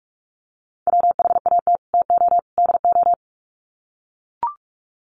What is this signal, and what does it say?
Morse code, heard as a keyed tone, about 44 wpm, 700 Hz. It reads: WHAT TO DO
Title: Sets of 3 Words 45wpm